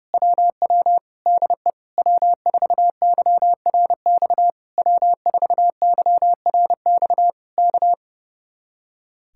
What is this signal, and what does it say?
Morse code, heard as a keyed tone, about 28 wpm, 700 Hz. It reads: WW DE W4YRX W4YRX K
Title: Speed-Racing Common QSO Phrases - 30wpm